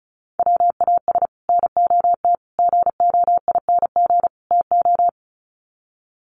Morse code, heard as a keyed tone, about 34 wpm, 700 Hz. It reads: WAS NOT GOING TO
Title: Sets of 4 Words 35wpm